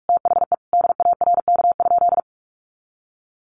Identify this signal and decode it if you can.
Morse code, heard as a keyed tone, about 44 wpm, 700 Hz. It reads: THE DARK?
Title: Sentences from Top 500 Words 45wpm